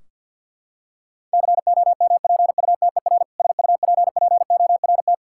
Morse code, heard as a keyed tone, about 50 wpm, 700 Hz. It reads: COMPUTER SUPPORT
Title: K1USN SST QSO Practice 50wpm